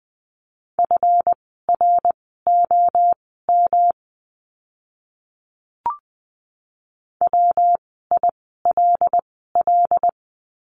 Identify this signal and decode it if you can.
Morse code, heard as a keyed tone, about 20 wpm, spaced as for 14 wpm, 700 Hz. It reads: FROM WILL
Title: 139 Lesson 23f - Letter f - Words 20wpm